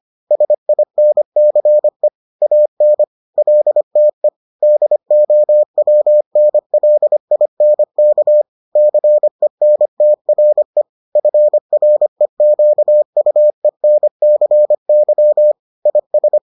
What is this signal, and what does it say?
Morse code, heard as a keyed tone, about 25 wpm, 600 Hz. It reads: SINCE AN LTE DOWNLINK CENTRE FREQUENCY IS